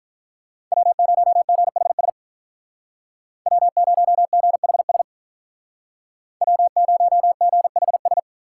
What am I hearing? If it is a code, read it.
Morse code, heard as a keyed tone, about 46 wpm, 700 Hz. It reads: W0GHS W0GHS W0GHS